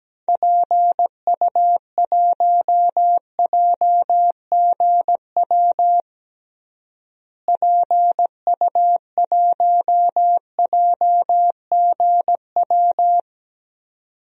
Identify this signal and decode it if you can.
Morse code, heard as a keyed tone, about 17 wpm, 700 Hz. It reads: PU1JGW PU1JGW
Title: DXCC-100 Call Signs - 6 Characters - Repeated 2x - 17wpm